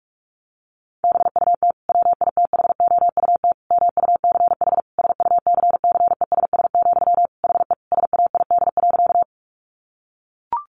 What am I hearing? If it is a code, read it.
Morse code, heard as a keyed tone, about 44 wpm, 700 Hz. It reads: BUT WITHOUT MUCH SUCCESS, HE SAID.
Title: Sentences from Top 900 Words 45wpm